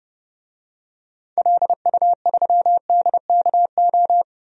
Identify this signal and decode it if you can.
Morse code, heard as a keyed tone, about 30 wpm, 700 Hz. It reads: LU3DKO
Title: DXCC-100 Call Signs - 6 Characters - 30wpm